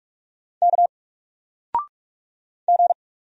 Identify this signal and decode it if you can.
Morse code, heard as a keyed tone, about 44 wpm, 700 Hz. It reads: K G
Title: Single Letter-Number 45wpm